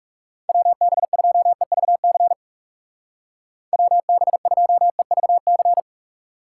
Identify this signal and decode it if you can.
Morse code, heard as a keyed tone, about 36 wpm, 700 Hz. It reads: WB2EVC WB2EVC
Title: Speed Racing - USA 2x3 Call Signs 30wpm